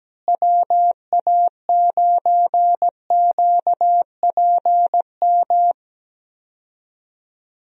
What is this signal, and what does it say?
Morse code, heard as a keyed tone, about 17 wpm, 700 Hz. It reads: WA9QPM
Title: USA Call Signs - 6 Characters - Repeated 3x - 17wpm